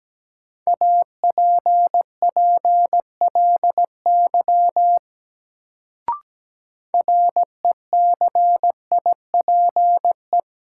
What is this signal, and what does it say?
Morse code, heard as a keyed tone, about 17 wpm, 700 Hz. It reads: APPLY RECIPE
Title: Sentences from Top 1200 Words - New Word Drill Alt - 17wpm